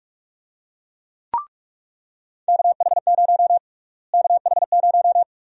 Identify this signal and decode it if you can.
Morse code, heard as a keyed tone, about 44 wpm, 700 Hz. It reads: KH0 KH0
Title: USA Call Sign Prefixes - Repeated 2x - 45wpm